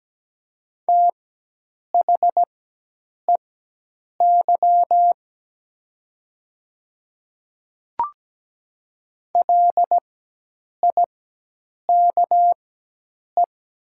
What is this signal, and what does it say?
Morse code, heard as a keyed tone, about 17 wpm, spaced as for 8 wpm, 700 Hz. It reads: THEY LIKE
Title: 4x Character Spacing - Top 100 Words - 17wpm